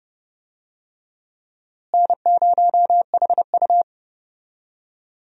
Morse code, heard as a keyed tone, about 30 wpm, 700 Hz. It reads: N0HU
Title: USA Call Signs - 4 Characters - Repeated 2x - 30wpm